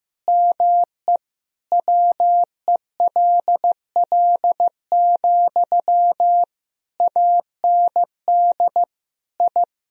Morse code, heard as a keyed tone, about 15 wpm, 700 Hz. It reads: ME WELL, AND I
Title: Sentences from Top 1000 Words - Review Part 2 - 15wpm